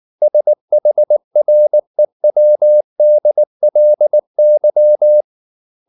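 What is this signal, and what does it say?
Morse code, heard as a keyed tone, about 19 wpm, 600 Hz. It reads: SHREWDLY
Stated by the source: Morse code at 19 wpm from CW med Gnister, reading Ken Follett's novel Never in English.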